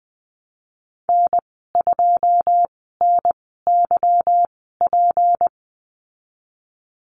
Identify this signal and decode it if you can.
Morse code, heard as a keyed tone, about 20 wpm, spaced as for 15 wpm, 700 Hz. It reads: N2NYP
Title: 173 Lesson 28g - Number 9 - Call Signs 20wpm